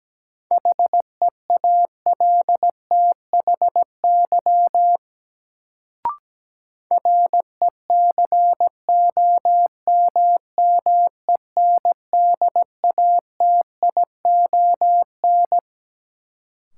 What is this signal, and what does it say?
Morse code, heard as a keyed tone, about 17 wpm, 700 Hz. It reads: HEALTHY RECOMMENDATION
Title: Sentences from Top 1400 Words - New Word Drill Alt - 17wpm